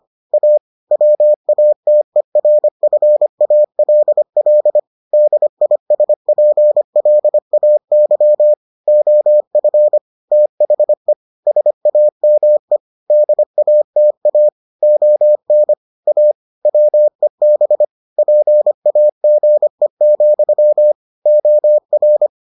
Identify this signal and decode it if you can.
Morse code, heard as a keyed tone, about 25 wpm, 600 Hz. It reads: A WATERFALL DISPLAY OF THE SAME DATA ON A WEB PAGE, OR